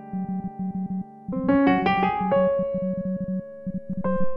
Morse code, heard as a keyed tone, about 32 wpm, 190 Hz. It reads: GO HOME. IF
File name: Cooper-Moore / A Retrospective 1990-2010 / Cooper-Moore - 05 - e If Only piano and morse code